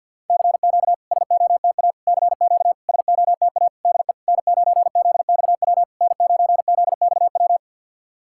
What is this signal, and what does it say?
Morse code, heard as a keyed tone, about 52 wpm, 690 Hz. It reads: CQ SOTA CQ SOTA DE N9ZXW N9ZXW
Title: SOTA QSO Practice 50wpm